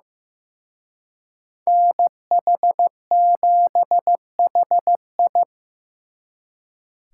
Morse code, heard as a keyed tone, about 18 wpm, 700 Hz. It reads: NH7HI
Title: Speed Racing - USA Call Signs 15wpm